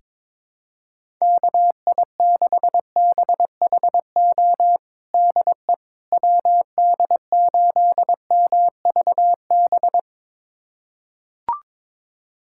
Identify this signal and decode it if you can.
Morse code, heard as a keyed tone, about 22 wpm, 700 Hz. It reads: KI6BHO DE WD8MVB
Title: USA Callsign DE Callsign 22wpm